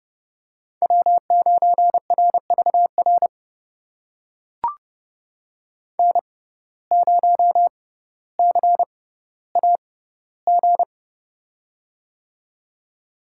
Morse code, heard as a keyed tone, about 30 wpm, spaced as for 15 wpm, 700 Hz. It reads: W9RVR N0CAG
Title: USA Call Signs - 6x Character Spacing - 30wpm